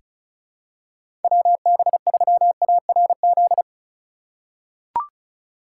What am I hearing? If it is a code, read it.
Morse code, heard as a keyed tone, about 34 wpm, 700 Hz. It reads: WB3ARZ
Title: USA Call Signs - 6 Characters - 3x Character Spacing - 35wpm